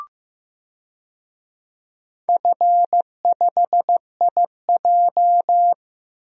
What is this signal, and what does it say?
Morse code, heard as a keyed tone, about 15 wpm, 700 Hz. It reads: F5IJ
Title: DXCC-100 Call Signs - 4 Characters - Repeated 2x - 15wpm